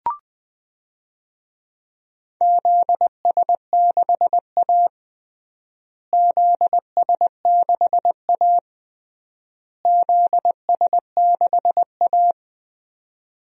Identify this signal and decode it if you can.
Morse code, heard as a keyed tone, about 21 wpm, 700 Hz. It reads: ZS6A ZS6A ZS6A
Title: DXCC-100 Call Signs - 4 Characters - Repeated 3x - 20wpm